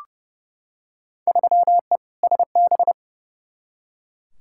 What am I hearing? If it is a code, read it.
Morse code, heard as a keyed tone, about 31 wpm, 700 Hz. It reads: 3E SB
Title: Field Day Exchanges Encore - 30wpm